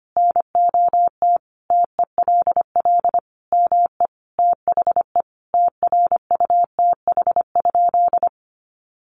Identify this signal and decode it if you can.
Morse code, heard as a keyed tone, about 25 wpm, 700 Hz. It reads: NOT TELL ME THE TRUTH?
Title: Sentences from Top 800 Words 25wpm